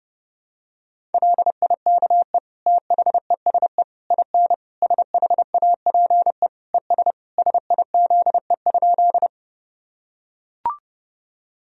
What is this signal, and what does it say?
Morse code, heard as a keyed tone, about 30 wpm, 700 Hz. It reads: LIKE THESE IN SHAPE ES SIZE?
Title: Sentences from Top 500 Words and Abbreviations 30wpm